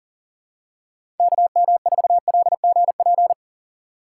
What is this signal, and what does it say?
Morse code, heard as a keyed tone, about 40 wpm, 700 Hz. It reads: KM4LGP
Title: USA Call Signs - 4x Character Spacing - 40wpm